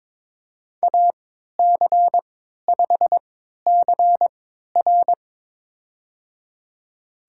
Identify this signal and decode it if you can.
Morse code, heard as a keyed tone, about 22 wpm, spaced as for 12 wpm, 700 Hz. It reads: AC5CR
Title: USA Call Signs - 3x Character Spacing - 22wpm